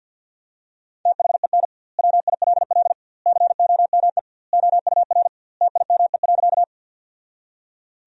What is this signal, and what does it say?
Morse code, heard as a keyed tone, about 51 wpm, 700 Hz. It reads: THEN WILL COME OUR TIME.
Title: 2x Word-Spacing - Sentences from Top 200 Words - 50wpm